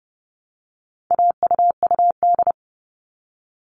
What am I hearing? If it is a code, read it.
Morse code, heard as a keyed tone, about 30 wpm, 700 Hz. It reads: AUUD
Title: All Cut Numbers - 4-Digit 30wpm